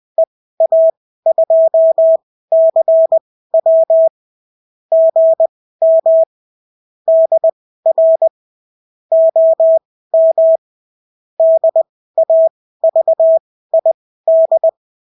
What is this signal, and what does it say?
Morse code, heard as a keyed tone, about 20 wpm, spaced as for 14 wpm, 650 Hz. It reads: EA2CW GM DR OM DAVID
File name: CW QSO libroCW4 20-10ppm